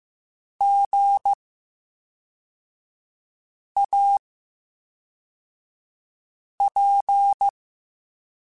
Morse code, heard as a keyed tone, about 15 wpm, 800 Hz. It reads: G A P
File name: Special INGs / Special INGs-mp3s / VerySlowToFaster Ing @10